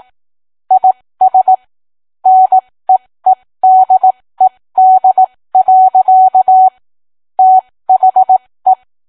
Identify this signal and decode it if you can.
Morse code, heard as a keyed tone, about 18 wpm, spaced as for 15 wpm, 750 Hz. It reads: IS NEEDED. THE